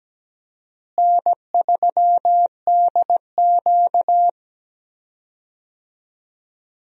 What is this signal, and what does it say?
Morse code, heard as a keyed tone, about 17 wpm, 700 Hz. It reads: N3DQ
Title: USA Call Signs - 4 Characters - Repeated 2x - 17wpm